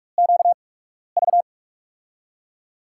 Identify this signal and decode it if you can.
Morse code, heard as a keyed tone, about 46 wpm, spaced as for 12 wpm, 700 Hz. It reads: QV
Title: USA Call Signs - 6 Characters - 8x Character Spacing - 45wpm